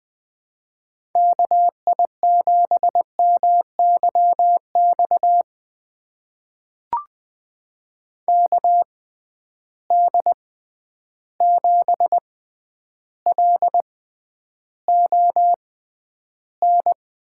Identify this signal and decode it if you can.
Morse code, heard as a keyed tone, about 20 wpm, spaced as for 10 wpm, 700 Hz. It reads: KI7MYX KD7LON
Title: USA Call Signs - 6 Characters - 6x Character Spacing - 20wpm